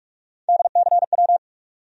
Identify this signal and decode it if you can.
Morse code, heard as a keyed tone, about 47 wpm, 700 Hz. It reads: DCW
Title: USA Call Sign Postfixes - 3 Letters 45wpm